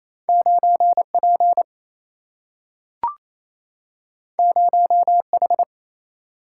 Morse code, heard as a keyed tone, about 28 wpm, 700 Hz. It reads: 9P 0H
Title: Single-Letter Single-Number 28wpm